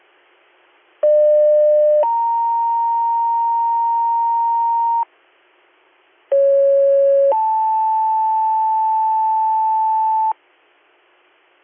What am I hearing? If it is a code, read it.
Two-tone page, heard as a one-second tone, then a three-second tone, about 600 then 930 Hz, about 550 then 880 Hz. It is heard from 1.0 to 10.3 s.